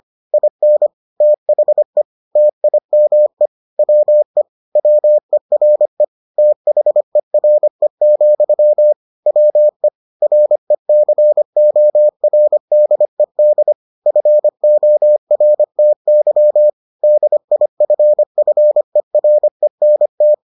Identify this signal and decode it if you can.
Morse code, heard as a keyed tone, about 25 wpm, 600 Hz. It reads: IN THE TIME WE WERE THERE, WE RECORDED FORTY DIFFERENT